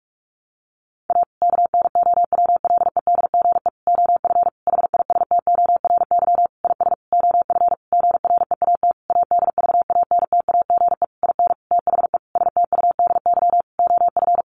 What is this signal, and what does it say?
Morse code, heard as a keyed tone, about 44 wpm, 700 Hz. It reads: A KNOWLEDGE OF HISTORY IS OF GREAT ADVANTAGE IN THE STUDY OF
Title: Sentences from Top 900 Words 45wpm